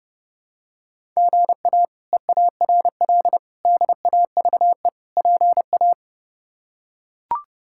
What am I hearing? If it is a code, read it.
Morse code, heard as a keyed tone, about 30 wpm, 700 Hz. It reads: GA EARL DAVE PA
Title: K1USN SST QSO Practice - 30wpm